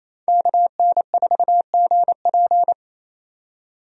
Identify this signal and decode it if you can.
Morse code, heard as a keyed tone, about 28 wpm, 700 Hz. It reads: KN4GP
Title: USA Call Signs - 5 Characters - 28wpm